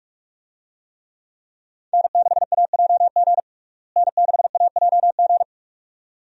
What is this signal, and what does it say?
Morse code, heard as a keyed tone, about 41 wpm, 700 Hz. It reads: N6AJG N6AJG
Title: USA Call Signs - 5 Characters - Repeated 2x - 45wpm